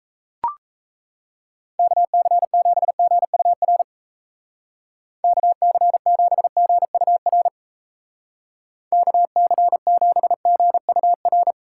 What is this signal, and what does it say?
Morse code, heard as a keyed tone, about 35 wpm, 700 Hz. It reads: KC7GUR KC7GUR KC7GUR
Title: Speed Racing - USA 2x3 Call Signs 28wpm